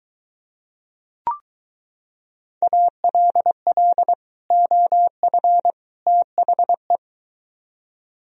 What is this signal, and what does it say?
Morse code, heard as a keyed tone, about 23 wpm, 700 Hz. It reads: ALL OF THE